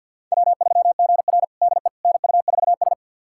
Morse code, heard as a keyed tone, about 53 wpm, 700 Hz. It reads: W3GR DE NU4S
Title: USA Callsign DE Callsign - 4 Characters - 50wpm